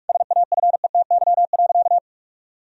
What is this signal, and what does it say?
Morse code, heard as a keyed tone, about 43 wpm, 700 Hz. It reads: SAFETY.
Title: Sentences from Top 1000 Words - Review Part 4 - 45wpm